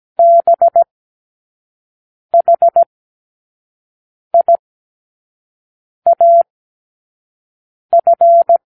Morse code, heard as a keyed tone, about 17 wpm, spaced as for 5 wpm, 680 Hz. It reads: BHIAF
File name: k7qo / Lesson18(A-L)